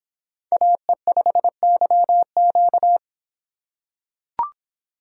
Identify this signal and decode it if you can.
Morse code, heard as a keyed tone, about 25 wpm, 700 Hz. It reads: AE5YQ